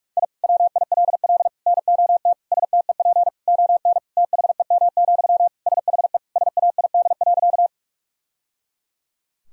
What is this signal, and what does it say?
Morse code, heard as a keyed tone, about 46 wpm, 700 Hz. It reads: I WILL NOT STEP ON THEM, SHE SAID.
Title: Sentences from Top 1000 Words - Review Part 2 - 45wpm